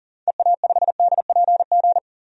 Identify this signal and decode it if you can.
Morse code, heard as a keyed tone, about 37 wpm, 700 Hz. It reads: EA5DPG
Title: DXCC-100 Call Signs - 6 Characters - 40wpm